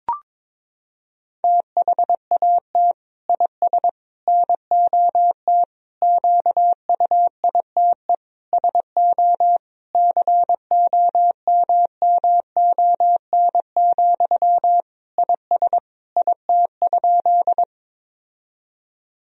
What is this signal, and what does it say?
Morse code, heard as a keyed tone, about 21 wpm, 700 Hz. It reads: THAT IS NOT QUITE SO COMMON, IS IT?
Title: Sentences from Top 1000 Words - Review Part 3 - 22wpm